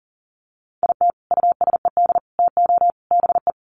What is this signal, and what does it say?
Morse code, heard as a keyed tone, about 37 wpm, 700 Hz. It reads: IT USED TO BE
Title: Sets of 4 Words 40wpm